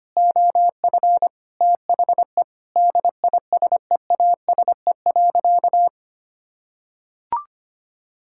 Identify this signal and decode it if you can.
Morse code, heard as a keyed tone, about 25 wpm, 700 Hz. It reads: OF THE DISEASE.
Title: Sentences from Top 1000 Words - Review Part 6 - 25wpm